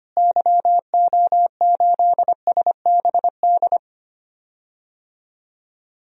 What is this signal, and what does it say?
Morse code, heard as a keyed tone, about 25 wpm, 700 Hz. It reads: YO8SBD
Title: DXCC-100 Call Signs - 6 Characters - Repeated 3x - 25wpm